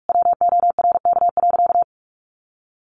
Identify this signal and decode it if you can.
Morse code, heard as a keyed tone, about 44 wpm, 700 Hz. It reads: WORK.
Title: Sentences from Top 1100 Words 45wpm